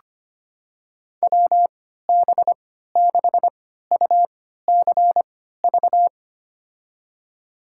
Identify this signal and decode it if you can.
Morse code, heard as a keyed tone, about 25 wpm, spaced as for 14 wpm, 700 Hz. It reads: WB6UCV
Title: USA Call Signs - 6 Characters - 3x Character Spacing - 25wpm